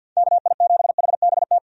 Morse code, heard as a keyed tone, about 52 wpm, 700 Hz. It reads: KI7HBT
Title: USA Call Signs - 6 Characters - 50wpm